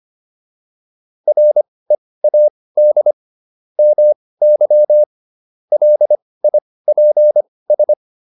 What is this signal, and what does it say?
Morse code, heard as a keyed tone, about 25 wpm, spaced as for 18 wpm, 600 Hz. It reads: READ MY LIPS